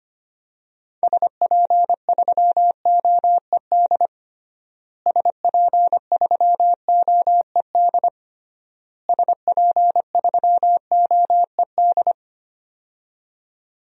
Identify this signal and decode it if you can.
Morse code, heard as a keyed tone, about 25 wpm, 700 Hz. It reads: SP3OED SP3OED SP3OED
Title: DXCC-100 Call Signs - 6 Characters - Repeated 3x - 25wpm